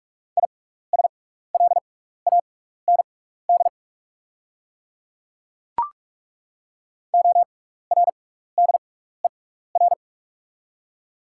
Morse code, heard as a keyed tone, about 43 wpm, spaced as for 15 wpm, 700 Hz. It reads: ISLAND ORDER
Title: 6x Character Spacing - Top 500 Words - 45wpm